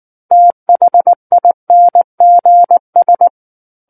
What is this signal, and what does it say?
Morse code, heard as a keyed tone, about 19 wpm, 700 Hz. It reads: THINGS
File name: Book One / 10-Chapter 10 - 19 WPM